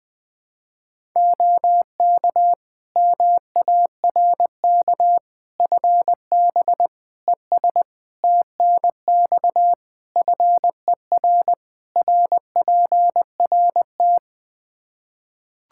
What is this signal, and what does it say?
Morse code, heard as a keyed tone, about 21 wpm, 700 Hz. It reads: OK MARK FB ES TNX FER RPRT
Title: Practice QSO - Exchange 2 - Info - 20wpm